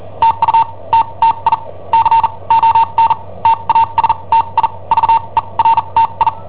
Morse code, heard as a keyed tone, about 43 wpm, spaced as for 30 wpm, 910 Hz. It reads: TUTTI CON TASTI VERTI
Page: Cassetta esercizi preparata da I7OHP per IW5ELP